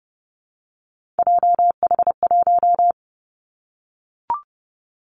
Morse code, heard as a keyed tone, about 30 wpm, 700 Hz. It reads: JH1
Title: Callsign Prefixes for Top 25 DXCC - 30wpm